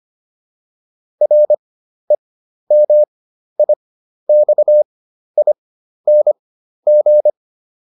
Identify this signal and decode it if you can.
Morse code, heard as a keyed tone, about 25 wpm, spaced as for 12 wpm, 600 Hz. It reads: REMIXING